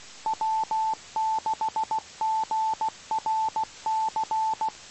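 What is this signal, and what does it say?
Morse code, heard as a keyed tone, about 16 wpm, 880 Hz. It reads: W6GRC
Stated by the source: Morse recorded on the 2 m band at 147.105 MHz.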